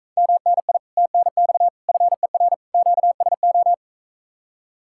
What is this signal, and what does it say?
Morse code, heard as a keyed tone, about 44 wpm, 700 Hz. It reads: MNI TNX FER QSO